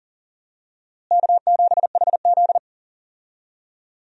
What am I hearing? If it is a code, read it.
Morse code, heard as a keyed tone, about 40 wpm, 700 Hz. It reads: K7HZ
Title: USA Call Signs - 4 Characters - Repeated 3x - 40wpm